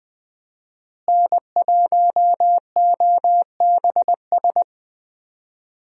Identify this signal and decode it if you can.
Morse code, heard as a keyed tone, about 20 wpm, 700 Hz. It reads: N1OBS
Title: USA Call Signs - 5 Characters - Repeated 3x - 20wpm